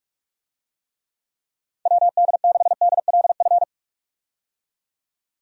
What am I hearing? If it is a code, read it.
Morse code, heard as a keyed tone, about 46 wpm, 700 Hz. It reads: WD6DLF
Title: USA Call Signs - 6 Characters - 45wpm